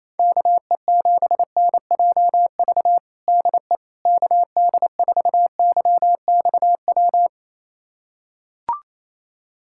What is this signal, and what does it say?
Morse code, heard as a keyed tone, about 27 wpm, 700 Hz. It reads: KE7NJV DE KD4YXW
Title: USA Callsign DE Callsign 28wpm